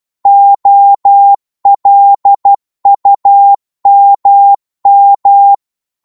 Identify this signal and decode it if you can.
Morse code, heard as a keyed tone, about 12 wpm, 800 Hz. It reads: OLUMM
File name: Morse / ltr-12wpm